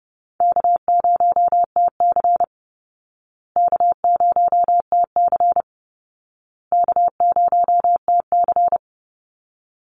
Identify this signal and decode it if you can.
Morse code, heard as a keyed tone, about 30 wpm, 700 Hz. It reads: K0TC K0TC K0TC
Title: Call Signs Repeated 2x - 30wpm